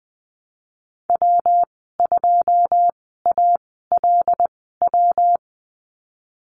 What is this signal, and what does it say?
Morse code, heard as a keyed tone, about 20 wpm, spaced as for 15 wpm, 700 Hz. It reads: W2ALW
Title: 122 Lesson 20g - Number 3 - Call Signs 20wpm